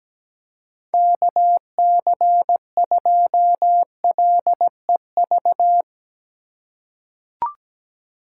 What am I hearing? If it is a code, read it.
Morse code, heard as a keyed tone, about 17 wpm, 700 Hz. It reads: KC2LEV
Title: USA Call Signs - 6 Characters - 4x Character Spacing - 17wpm